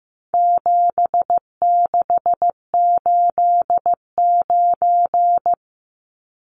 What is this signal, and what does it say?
Morse code, heard as a keyed tone, about 15 wpm, 700 Hz. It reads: 7689
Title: Numbers - 4-Digit 15wpm